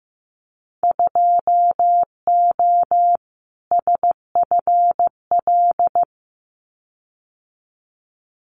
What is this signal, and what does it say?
Morse code, heard as a keyed tone, about 15 wpm, 700 Hz. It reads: 2O SFL
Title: Winter Field Day Exchanges 15wpm